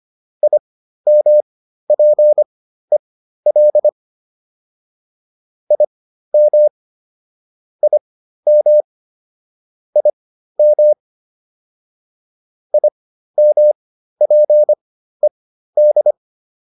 Morse code, heard as a keyed tone, about 25 wpm, spaced as for 13 wpm, 600 Hz. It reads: IMPEL IM IM IM IMPED